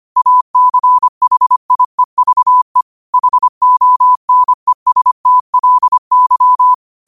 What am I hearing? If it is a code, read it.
Morse code, heard as a keyed tone, about 25 wpm, 1000 Hz. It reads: ACHIEVE HONESTLY
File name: qotdfiles / 25wpm